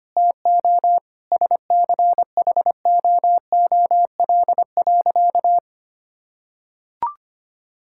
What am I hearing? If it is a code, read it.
Morse code, heard as a keyed tone, about 25 wpm, 700 Hz. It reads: TO SCHOOL.
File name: practice / 25wpm / Sentences with 4-Word Phrases - 25wpm